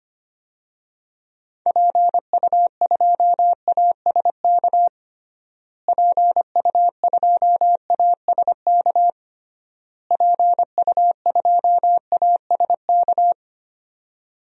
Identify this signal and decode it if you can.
Morse code, heard as a keyed tone, about 25 wpm, 700 Hz. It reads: PU2ASK PU2ASK PU2ASK
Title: DXCC-100 Call Signs - 6 Characters - Repeated 3x - 25wpm